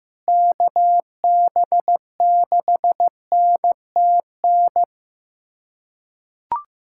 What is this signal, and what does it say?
Morse code, heard as a keyed tone, about 15 wpm, 700 Hz. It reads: KB6NTN
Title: USA Callsign DE Callsign 15wpm